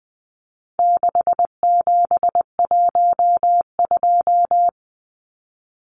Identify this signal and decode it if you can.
Morse code, heard as a keyed tone, about 19 wpm, 700 Hz. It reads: 6712
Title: Numbers - 4-Digit 20wpm